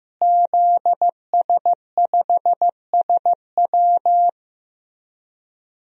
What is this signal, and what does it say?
Morse code, heard as a keyed tone, about 15 wpm, 700 Hz. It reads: ZS5SW
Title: DXCC-100 Call Signs - 5 Characters - Repeated 2x - 15wpm